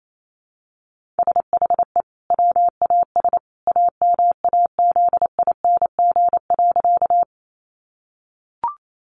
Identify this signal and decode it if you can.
Morse code, heard as a keyed tone, about 28 wpm, 700 Hz. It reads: SHE WAS AMAZING.
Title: Sentences from Top 1500 Words 28wpm